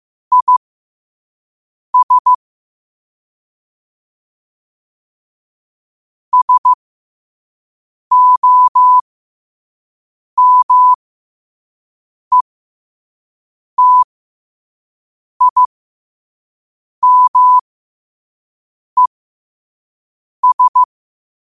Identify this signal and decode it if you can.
Morse code, heard as a keyed tone, about 15 wpm, spaced as for 5 wpm, 1000 Hz. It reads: IS SOMETIMES